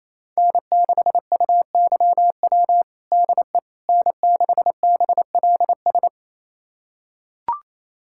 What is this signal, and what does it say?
Morse code, heard as a keyed tone, about 28 wpm, 700 Hz. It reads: N6UYW DE N6BLS
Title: USA Callsign DE Callsign - 5 Characters - 28wpm